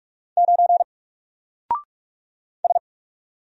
Morse code, heard as a keyed tone, about 44 wpm, 700 Hz. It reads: 9 S
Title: Single Character (Letter:Number:Punctuation:Prosign) 45wpm